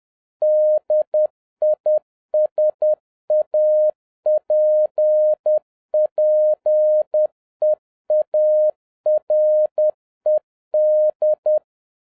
Morse code, heard as a keyed tone, about 10 wpm, 610 Hz. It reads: DISAPPEARED